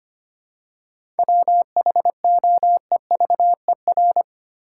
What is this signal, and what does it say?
Morse code, heard as a keyed tone, about 25 wpm, 700 Hz. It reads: WHOEVER
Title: Seven Letter Words - 25wpm